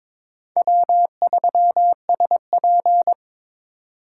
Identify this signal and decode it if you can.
Morse code, heard as a keyed tone, about 26 wpm, 700 Hz. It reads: W3SP
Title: Speed Racing - USA 1x2 Call Signs 22wpm